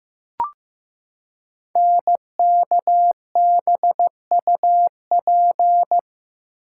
Morse code, heard as a keyed tone, about 15 wpm, 700 Hz. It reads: NKBUP
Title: Code Groups - 5 Letters - 15wpm